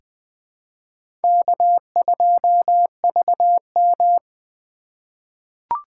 Morse code, heard as a keyed tone, about 20 wpm, 700 Hz. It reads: K2VM
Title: USA Call Signs - 2x Character Spacing - 20wpm